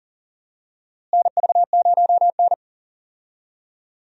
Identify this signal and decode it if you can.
Morse code, heard as a keyed tone, about 40 wpm, 700 Hz. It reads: NV0N
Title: USA Call Signs - 4 Characters - Repeated 3x - 40wpm